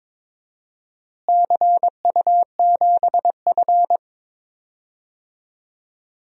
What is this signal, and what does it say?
Morse code, heard as a keyed tone, about 22 wpm, 700 Hz. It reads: CU7F
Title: DXCC-100 Call Signs - 4 Characters - Repeated 2x - 22wpm